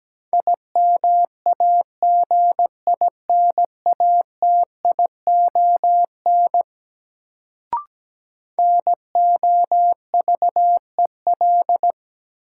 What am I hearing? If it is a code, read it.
Morse code, heard as a keyed tone, about 17 wpm, 700 Hz. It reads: IMAGINATION NOVEL
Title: Sentences from Top 1300 Words - New Word Drill - 17wpm